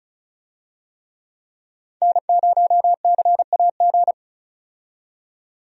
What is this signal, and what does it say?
Morse code, heard as a keyed tone, about 36 wpm, 700 Hz. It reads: N0CAG
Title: USA Call Signs - 5 Characters - 35wpm